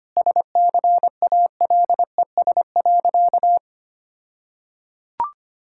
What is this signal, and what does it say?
Morse code, heard as a keyed tone, about 25 wpm, 700 Hz. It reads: SCALES.
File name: practice / 25wpm / All-Age Appropriate Jokes - 25wpm